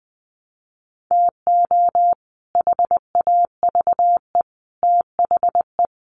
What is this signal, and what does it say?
Morse code, heard as a keyed tone, about 19 wpm, 700 Hz. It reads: TO HAVE THE
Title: Sets of 3 Words 20wpm